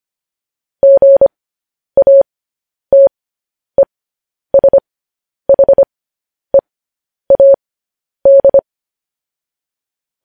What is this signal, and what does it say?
Morse code, heard as a keyed tone, about 25 wpm, spaced as for 10 wpm, 550 Hz. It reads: GATESHEAD
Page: News Headlines in Morse Code at 10 WPM